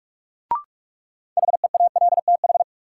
Morse code, heard as a keyed tone, about 46 wpm, 700 Hz. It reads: HEALTH
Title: Sentences from Top 600 Words - New Word Drill - 45wpm